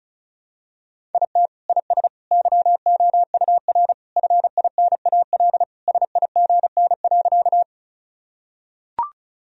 Morse code, heard as a keyed tone, about 36 wpm, 700 Hz. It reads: IT IS YOUR FINAL SIGN.
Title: Sentences from Top 1000 Words - Review Part 2 - 35wpm